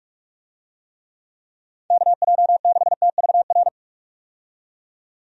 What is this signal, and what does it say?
Morse code, heard as a keyed tone, about 45 wpm, 700 Hz. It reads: KJ6TVR
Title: USA Call Signs - 6 Characters - 45wpm